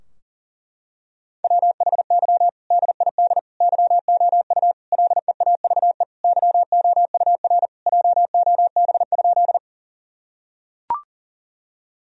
Morse code, heard as a keyed tone, about 40 wpm, 700 Hz. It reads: WHY DID YOU LEAVE YOUR JOB?
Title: Sentences from Top 1000 Words - Review Part 3 - 40wpm